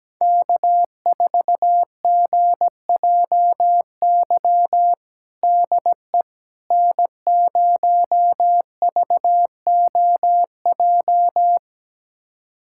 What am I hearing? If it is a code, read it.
Morse code, heard as a keyed tone, about 17 wpm, 700 Hz. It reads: K4GJY DE N0VOJ
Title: USA Callsign DE Callsign - 5 Characters - 17wpm